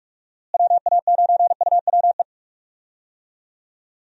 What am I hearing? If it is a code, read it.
Morse code, heard as a keyed tone, about 46 wpm, 700 Hz. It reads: WA9UWE
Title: USA Call Signs - 6 Characters - Repeated 3x - 45wpm